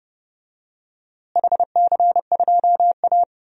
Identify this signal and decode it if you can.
Morse code, heard as a keyed tone, about 31 wpm, 700 Hz. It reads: HC2A
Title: DXCC-100 Call Signs - 4 Characters - Repeated 3x - 30wpm